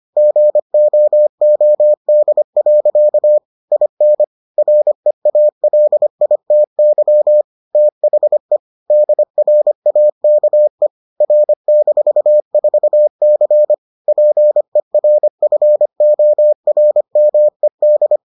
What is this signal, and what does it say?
Morse code, heard as a keyed tone, about 25 wpm, 600 Hz. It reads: GOOD. IN REALITY THE DRAKE R-4C PERFORMED